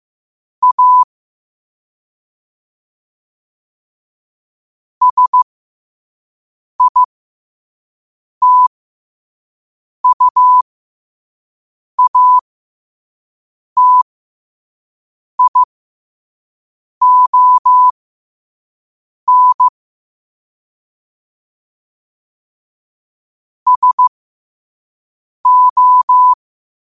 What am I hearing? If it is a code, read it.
Morse code, heard as a keyed tone, about 15 wpm, spaced as for 5 wpm, 1000 Hz. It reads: A SITUATION SO
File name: qotdfiles / 5wpm